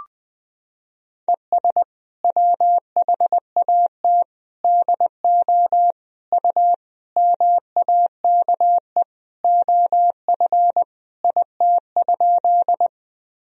Morse code, heard as a keyed tone, about 20 wpm, 700 Hz. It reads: ES WHAT DO U MAKE OF IT?
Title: Sentences from Top 100 Words and Abbreviations 20wpm